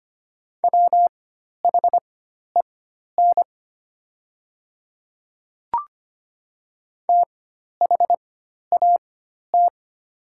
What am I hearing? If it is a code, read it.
Morse code, heard as a keyed tone, about 25 wpm, spaced as for 12 wpm, 700 Hz. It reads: WHEN THAT
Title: 4x Character Spacing - Top 100 Words - 25wpm